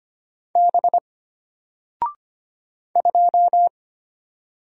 Morse code, heard as a keyed tone, about 25 wpm, 700 Hz. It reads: B 2
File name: Single-Letter-Number-25wpm